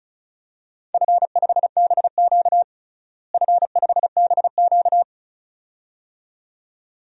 Morse code, heard as a keyed tone, about 36 wpm, 700 Hz. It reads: F5BQ F5BQ
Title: DXCC-100 Call Signs - 4 Characters - Repeated 2x - 35wpm